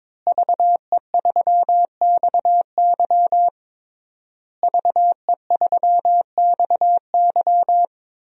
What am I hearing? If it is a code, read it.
Morse code, heard as a keyed tone, about 22 wpm, 700 Hz. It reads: VE3XY VE3XY
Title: DXCC-100 Call Signs - 5 Characters - Repeated 3x - 22wpm